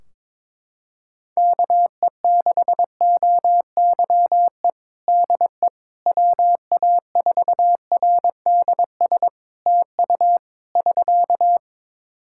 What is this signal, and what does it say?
Morse code, heard as a keyed tone, about 22 wpm, 700 Hz. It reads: KE6OYE DE WA4RDS TU <SK>
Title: Practice QSO - Exchange 4 - Ending - 22wpm